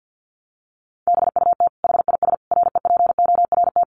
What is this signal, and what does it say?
Morse code, heard as a keyed tone, about 50 wpm, 700 Hz. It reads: BUT HIS REPORT
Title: Sentences from Top 1200 Words 50wpm